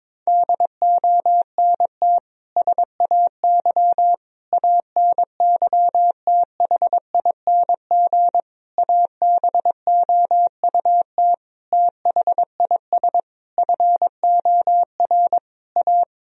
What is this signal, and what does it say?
Morse code, heard as a keyed tone, about 21 wpm, 700 Hz. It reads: DONT SAY ANYTHING ABOUT THIS FOR A
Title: Sentences from Top 1000 Words - Review Part 1 - 22wpm